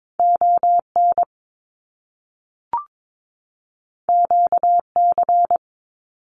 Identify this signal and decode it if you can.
Morse code, heard as a keyed tone, about 22 wpm, 700 Hz. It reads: ON QC
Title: Canadian Provinces and Territories 22wpm